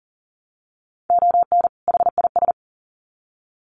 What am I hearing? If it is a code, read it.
Morse code, heard as a keyed tone, about 40 wpm, 700 Hz. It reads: ON HIS
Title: Sets of 2 Words 40wpm